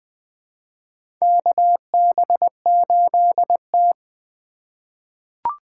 Morse code, heard as a keyed tone, about 21 wpm, 700 Hz. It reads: KB8T
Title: USA Call Signs - 4 Characters - 4x Character Spacing - 20wpm